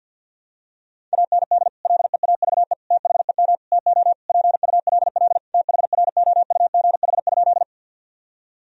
Morse code, heard as a keyed tone, about 51 wpm, 700 Hz. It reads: AND LEAVE THEM TO PULL THROUGH?
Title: Sentences from Top 1000 Words - Review Part 2 - 50wpm